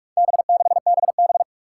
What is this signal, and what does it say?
Morse code, heard as a keyed tone, about 44 wpm, 700 Hz. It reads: B6BB
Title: Four Challenging Letter-Number Combinations 45wpm